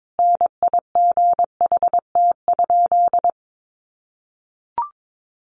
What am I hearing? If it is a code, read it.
Morse code, heard as a keyed tone, about 22 wpm, 700 Hz. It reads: NIGHT?
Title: Sentences from Top 600 Words 22wpm